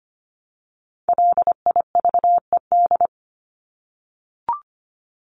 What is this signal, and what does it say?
Morse code, heard as a keyed tone, about 25 wpm, 700 Hz. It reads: LIVED
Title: ED Suffix - Top 100 Words with Base Words 25wpm